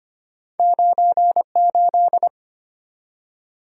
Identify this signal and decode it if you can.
Morse code, heard as a keyed tone, about 25 wpm, 700 Hz. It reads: 98
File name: Two Letter-Number Combinations - 25wpm